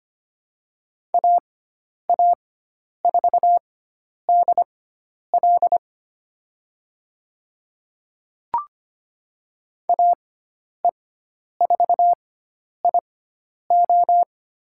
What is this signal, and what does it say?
Morse code, heard as a keyed tone, about 25 wpm, spaced as for 10 wpm, 700 Hz. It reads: AA4DL AE4IO
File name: Lesson 16g - Letter c - Call Signs 25wpm